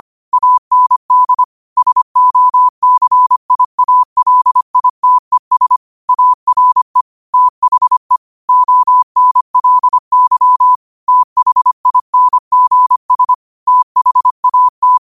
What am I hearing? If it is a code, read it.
Morse code, heard as a keyed tone, about 25 wpm, 1000 Hz. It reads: AND SOCIALITES ARE THE ONLY THINGS THAT
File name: qotdfiles / 25wpm